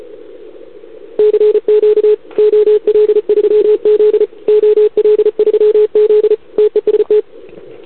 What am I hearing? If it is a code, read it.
Morse code, heard as a keyed tone, about 34 wpm, 440 Hz, with radio noise. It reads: CQ OL3Z OL3Z TEST
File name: CQWWCW2021 / OL3Z